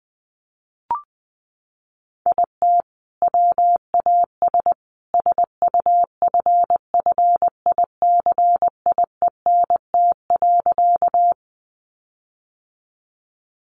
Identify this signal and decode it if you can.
Morse code, heard as a keyed tone, about 20 wpm, 700 Hz. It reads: IT WAS SUFFICIENT.
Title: Sentences from Top 1400 Words 20wpm